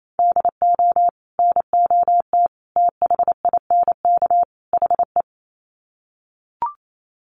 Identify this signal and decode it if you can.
Morse code, heard as a keyed tone, about 28 wpm, 700 Hz. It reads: DO NOT THINK HE
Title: Sets of 4 Words 28wpm